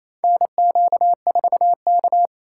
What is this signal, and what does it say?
Morse code, heard as a keyed tone, about 27 wpm, 700 Hz. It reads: NQ4K
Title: USA Call Signs - 4 Characters - 4x Character Spacing - 28wpm